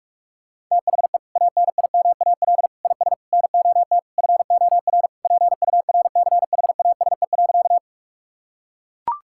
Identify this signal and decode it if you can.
Morse code, heard as a keyed tone, about 42 wpm, 690 Hz. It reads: THE ANIMAL IS NOT FOR PURCHASE.
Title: Sentences from Top 1000 Words - Review Part 5 - 45wpm